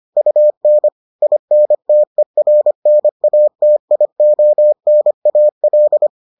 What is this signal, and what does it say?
Morse code, heard as a keyed tone, about 25 wpm, 600 Hz. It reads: UN INTERNATIONAL